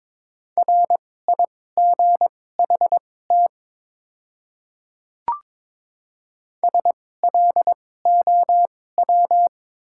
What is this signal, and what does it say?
Morse code, heard as a keyed tone, about 22 wpm, spaced as for 16 wpm, 700 Hz. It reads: RIGHT SLOW
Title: 2x Character Spacing - Top 500 Words - 22wpm